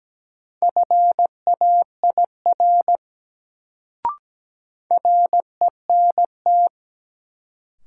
Morse code, heard as a keyed tone, about 17 wpm, 700 Hz. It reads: FAIR RENT
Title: Sentences from Top 1000 Words - Review Part 5 - New Word Drill - 17wpm